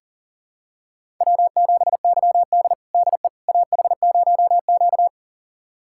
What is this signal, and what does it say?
Morse code, heard as a keyed tone, about 40 wpm, 700 Hz. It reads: W7YD DE AH0Q
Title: USA Callsign DE Callsign - 4 Characters - 40wpm